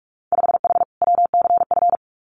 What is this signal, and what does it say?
Morse code, heard as a keyed tone, about 44 wpm, 700 Hz. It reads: 5H WCF
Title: Winter Field Day Exchanges Encore 45wpm